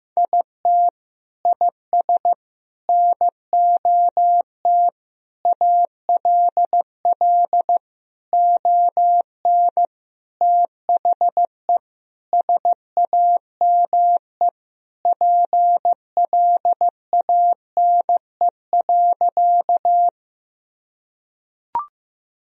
Morse code, heard as a keyed tone, about 15 wpm, 700 Hz. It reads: IT IS NOT ALL ON THE SAME PLANE.
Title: Sentences from Top 1000 Words - Review Part 3 - 15wpm